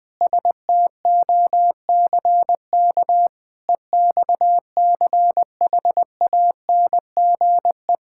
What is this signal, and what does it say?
Morse code, heard as a keyed tone, about 20 wpm, 700 Hz. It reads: STOCK EXCHANGE
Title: Sentences from Top 1000 Words - Review Part 6 - 20wpm